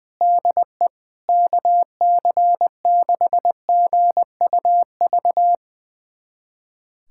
Morse code, heard as a keyed tone, about 20 wpm, 700 Hz. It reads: DE KC6GUV
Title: USA Callsign DE Callsign 20wpm